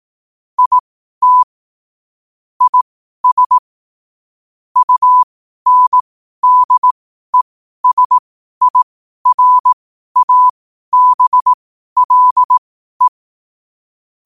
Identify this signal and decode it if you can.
Morse code, heard as a keyed tone, about 18 wpm, spaced as for 12 wpm, 1000 Hz. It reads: IT IS UNDESIRABLE
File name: qotdfiles / 12wpm